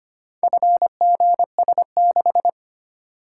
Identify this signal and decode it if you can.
Morse code, heard as a keyed tone, about 25 wpm, 700 Hz. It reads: FGS6
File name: practice / 25wpm / Four Letter-Number Combinations - 25wpm